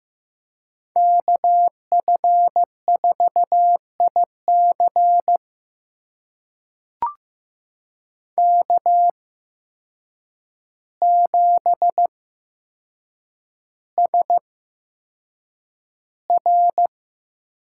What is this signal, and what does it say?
Morse code, heard as a keyed tone, about 15 wpm, spaced as for 7 wpm, 700 Hz. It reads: KF4IC K7SR
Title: USA Call Signs - 5 Characters - 8x Character Spacing - 15wpm